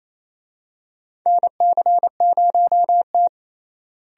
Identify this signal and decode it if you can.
Morse code, heard as a keyed tone, about 28 wpm, 700 Hz. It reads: NC0T
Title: USA Call Signs - 4 Characters - Repeated 3x - 28wpm